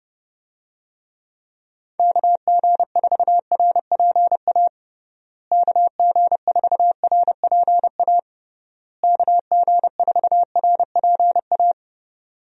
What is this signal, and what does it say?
Morse code, heard as a keyed tone, about 29 wpm, 700 Hz. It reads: KG4RPA KG4RPA KG4RPA
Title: USA Call Signs - 6 Characters - Repeated 3x - 30wpm